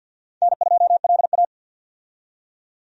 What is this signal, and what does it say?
Morse code, heard as a keyed tone, about 52 wpm, 700 Hz. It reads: N2LA
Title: USA Call Signs - 4 Characters - 50wpm